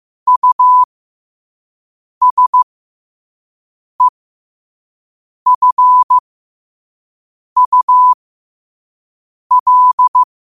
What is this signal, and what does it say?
Morse code, heard as a keyed tone, about 15 wpm, spaced as for 5 wpm, 1000 Hz. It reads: USEFUL
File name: qotdfiles / 5wpm